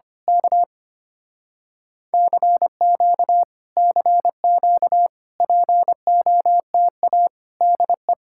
Morse code, heard as a keyed tone, about 24 wpm, 700 Hz. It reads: K CQ CQ POTA DE
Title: Speed-Racing Common QSO Phrases - 25wpm